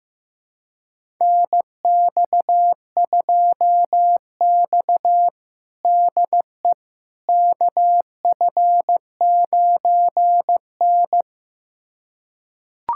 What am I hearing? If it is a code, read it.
Morse code, heard as a keyed tone, about 15 wpm, 700 Hz. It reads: NX2X DE KF9N
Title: USA Callsign DE Callsign - 4 Characters - 15wpm